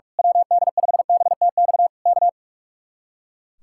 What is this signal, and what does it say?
Morse code, heard as a keyed tone, about 44 wpm, 700 Hz. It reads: WD5BTX K
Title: Speed-Racing Common QSO Phrases - 45wpm